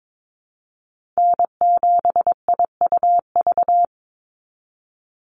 Morse code, heard as a keyed tone, about 22 wpm, 700 Hz. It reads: N7IUV
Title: USA Call Signs 22wpm